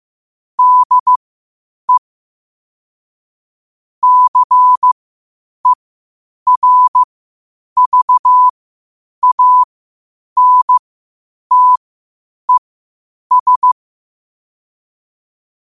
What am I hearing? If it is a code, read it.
Morse code, heard as a keyed tone, about 15 wpm, spaced as for 8 wpm, 1000 Hz. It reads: DE CERVANTES
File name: qotdfiles / 8wpm